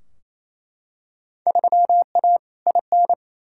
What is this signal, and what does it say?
Morse code, heard as a keyed tone, about 28 wpm, 700 Hz. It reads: 3A IN
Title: Field Day Exchanges Encore - 28wpm